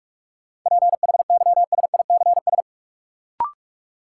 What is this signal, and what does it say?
Morse code, heard as a keyed tone, about 41 wpm, 700 Hz. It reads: PHYSICS
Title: Sentences from Top 1500 Words - New Word Drill Alt - 45wpm